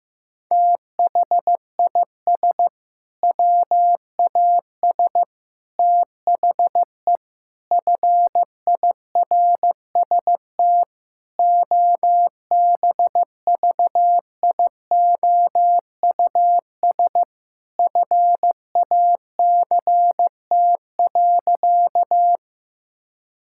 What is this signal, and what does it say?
Morse code, heard as a keyed tone, about 15 wpm, 700 Hz. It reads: THIS WAS THE FIRST OBVIOUS FACT.
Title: Sentences from Top 1600 Words 15wpm